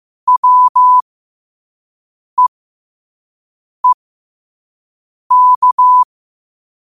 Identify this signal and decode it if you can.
Morse code, heard as a keyed tone, about 15 wpm, spaced as for 5 wpm, 1000 Hz. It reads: WEEK
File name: qotdfiles / 5wpm